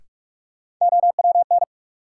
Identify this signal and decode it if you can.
Morse code, heard as a keyed tone, about 44 wpm, 700 Hz. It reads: OWN
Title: Sentences from Top 300 Words - New Word Drill - 45wpm